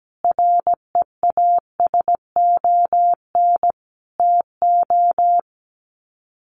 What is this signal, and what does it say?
Morse code, heard as a keyed tone, about 17 wpm, 700 Hz. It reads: REASON TO
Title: Sets of 5 Words Encore 17wpm